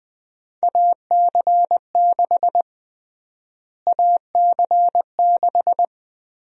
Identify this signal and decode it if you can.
Morse code, heard as a keyed tone, about 20 wpm, 700 Hz. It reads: AC6 AC6
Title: USA Call Sign Prefixes - Repeated 2x - 20wpm